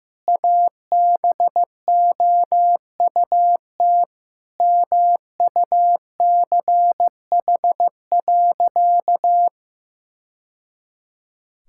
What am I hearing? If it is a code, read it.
Morse code, heard as a keyed tone, about 15 wpm, 700 Hz. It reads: ABOUT MUCH.
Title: Sentences from Top 1000 Words - Review Part 6 - 15wpm